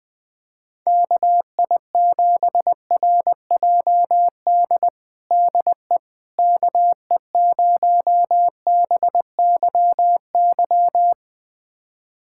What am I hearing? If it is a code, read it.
Morse code, heard as a keyed tone, about 20 wpm, 700 Hz. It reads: KI7RJD DE KE0BYY
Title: USA Callsign DE Callsign - 6 Characters - 20wpm